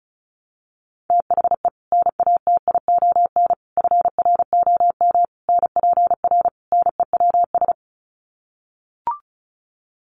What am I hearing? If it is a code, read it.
Morse code, heard as a keyed tone, about 34 wpm, 700 Hz. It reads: THE NATION FROM NPR NEWS
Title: Sets of 5 Words 35wpm